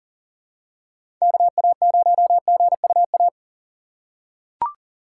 Morse code, heard as a keyed tone, about 40 wpm, 700 Hz. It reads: KA0GUA
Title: USA Call Signs - 6 Characters - 4x Character Spacing - 40wpm